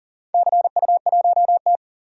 Morse code, heard as a keyed tone, about 40 wpm, 700 Hz. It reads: CU1T
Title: DXCC-100 Call Signs - 4 Characters - 40wpm